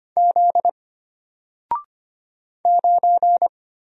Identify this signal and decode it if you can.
Morse code, heard as a keyed tone, about 25 wpm, 700 Hz. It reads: Z 9
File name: Single-Letter-Number-25wpm